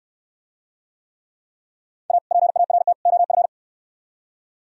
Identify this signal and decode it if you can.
Morse code, heard as a keyed tone, about 67 wpm, 700 Hz. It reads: A PART OF